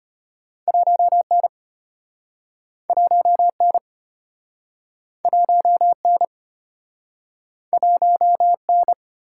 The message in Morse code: 1N 1N 1N 1N

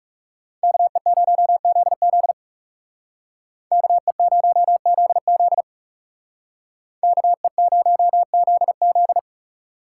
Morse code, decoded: KE0ZZ KE0ZZ KE0ZZ

